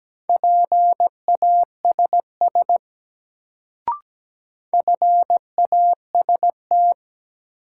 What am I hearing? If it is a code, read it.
Morse code, heard as a keyed tone, about 17 wpm, 700 Hz. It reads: PASS FAST
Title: Sentences from Top 400 Words - New Word Drill - 17wpm